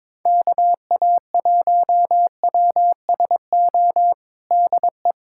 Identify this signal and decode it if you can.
Morse code, heard as a keyed tone, about 22 wpm, 700 Hz. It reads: KA1WSO DE
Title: USA Callsign DE Callsign - 6 Characters - 22wpm